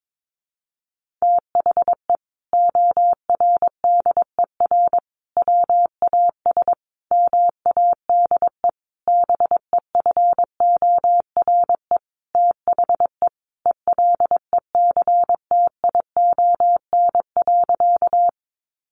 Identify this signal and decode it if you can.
Morse code, heard as a keyed tone, about 22 wpm, 700 Hz. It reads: THE ORDER WAS MADE BEFORE THE ELECTION.
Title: Sentences from Top 1500 Words 22wpm